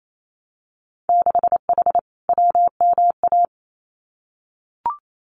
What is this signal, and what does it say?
Morse code, heard as a keyed tone, about 28 wpm, 700 Hz. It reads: 6H WMA
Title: Winter Field Day Exchanges Encore 28wpm